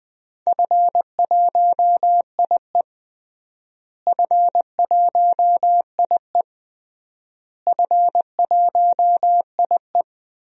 Morse code, heard as a keyed tone, about 20 wpm, 700 Hz. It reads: F1IE F1IE F1IE
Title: DXCC-100 Call Signs - 4 Characters - Repeated 3x - 20wpm